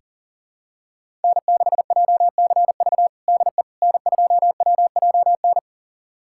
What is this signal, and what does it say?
Morse code, heard as a keyed tone, about 40 wpm, 700 Hz. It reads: N6JCV DE N2WJN
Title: USA Callsign DE Callsign - 5 Characters - 40wpm